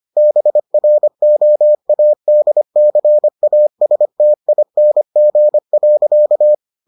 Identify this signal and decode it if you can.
Morse code, heard as a keyed tone, about 25 wpm, 600 Hz. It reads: BROADCASTING.